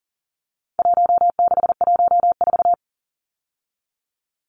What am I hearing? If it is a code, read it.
Morse code, heard as a keyed tone, about 40 wpm, 700 Hz. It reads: 1614
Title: Numbers - Contest Style - 4-Digit 40wpm